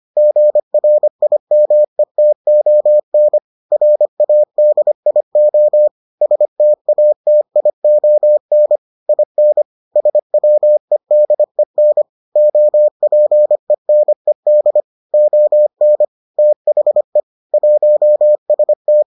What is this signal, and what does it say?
Morse code, heard as a keyed tone, about 25 wpm, 600 Hz. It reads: GRIMETON RADIO STATION IN SWEDEN OPENED ON THE 1ST